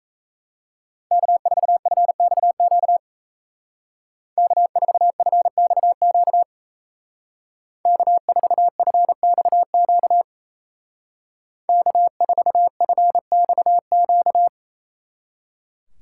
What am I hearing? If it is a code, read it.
Morse code, heard as a keyed tone, about 33 wpm, 700 Hz. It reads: K4FXQ K4FXQ K4FXQ K4FXQ
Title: Speed Racing - USA 1x3 Call Signs 28wpm